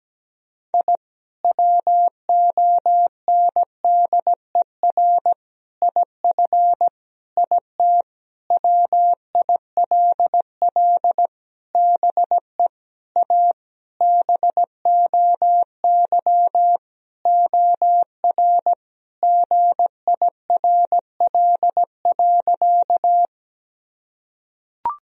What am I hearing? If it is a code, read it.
Morse code, heard as a keyed tone, about 17 wpm, 700 Hz. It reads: I WONDER IF IT WILL BE A BOY OR GIRL.
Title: Sentences from Top 1000 Words - Review Part 3 - 17wpm